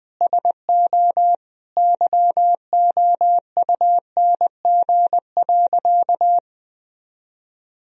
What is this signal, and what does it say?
Morse code, heard as a keyed tone, about 20 wpm, 700 Hz. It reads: SO YOUNG.
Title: Sentences from Top 400 Words and Abbreviations 20wpm